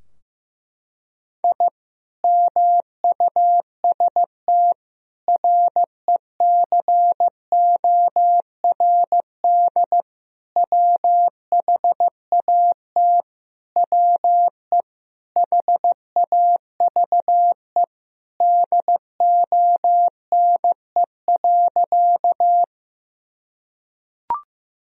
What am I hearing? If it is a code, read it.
Morse code, heard as a keyed tone, about 15 wpm, 700 Hz. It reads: I MUST RECORD WHAT WE HAVE DONE.
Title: Sentences from Top 1000 Words - Review Part 3 - 15wpm